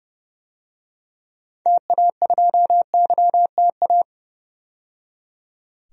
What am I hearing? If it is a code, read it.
Morse code, heard as a keyed tone, about 30 wpm, 700 Hz. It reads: TA2YTA
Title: DXCC-100 Call Signs - 6 Characters - 30wpm